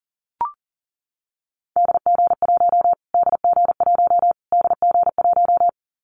Morse code, heard as a keyed tone, about 40 wpm, 700 Hz. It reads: DG1 DG1 DG1
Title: Callsign Prefixes for Top 25 DXCC - Repeated 3x - 40wpm